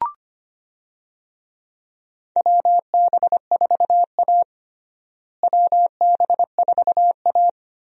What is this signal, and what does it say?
Morse code, heard as a keyed tone, about 25 wpm, 700 Hz. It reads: WB4A WB4A
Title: USA Call Signs - 4 Characters - Repeated 3x - 25wpm